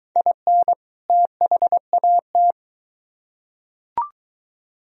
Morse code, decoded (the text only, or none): IN THAT